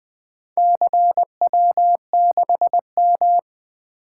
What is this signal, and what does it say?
Morse code, heard as a keyed tone, about 20 wpm, 700 Hz. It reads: CW6M
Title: Four Letter-Number Combinations - 20wpm